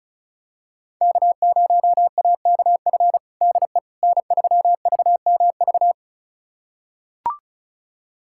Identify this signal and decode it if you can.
Morse code, heard as a keyed tone, about 36 wpm, 700 Hz. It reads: K0AKF DE N3VMV
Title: USA Callsign DE Callsign - 5 Characters - 35wpm